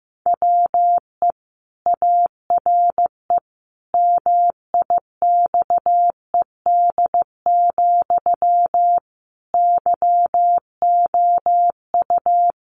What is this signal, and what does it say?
Morse code, heard as a keyed tone, about 15 wpm, 700 Hz. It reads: WE ARE MIXED, YOU
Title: Sentences from Top 1500 Words 15wpm